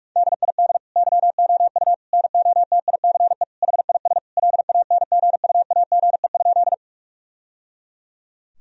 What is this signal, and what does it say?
Morse code, heard as a keyed tone, about 42 wpm, 700 Hz. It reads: DID YOU NOTICE HIS LANGUAGE?
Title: Sentences from Top 1000 Words - Review Part 3 - 45wpm